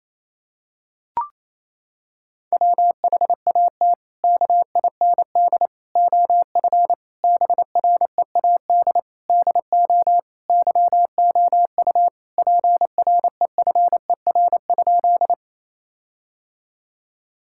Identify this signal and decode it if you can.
Morse code, heard as a keyed tone, about 28 wpm, 700 Hz. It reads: WHAT KIND OF BREAD DO YOU PREFER?
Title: Sentences from Top 1600 Words 28wpm